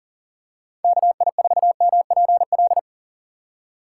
Morse code, heard as a keyed tone, about 40 wpm, 700 Hz. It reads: KI4MPL
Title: USA Call Signs - 6 Characters - 3x Character Spacing - 40wpm